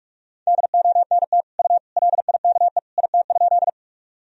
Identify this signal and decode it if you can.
Morse code, heard as a keyed tone, about 46 wpm, 700 Hz. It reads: DONT U LIKE IT?
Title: Sentences from Top 300 Words and Abbreviations 45wpm